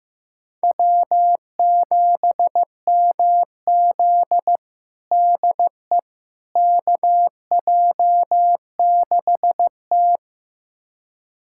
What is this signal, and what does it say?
Morse code, heard as a keyed tone, about 15 wpm, 700 Hz. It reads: W7MZ DE KJ6T
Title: USA Callsign DE Callsign - 4 Characters - 15wpm